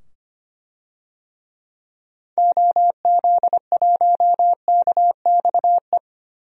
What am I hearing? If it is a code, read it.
Morse code, heard as a keyed tone, about 25 wpm, 700 Hz. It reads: OZ1KXE
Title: DXCC-100 Call Signs - 6 Characters - 25wpm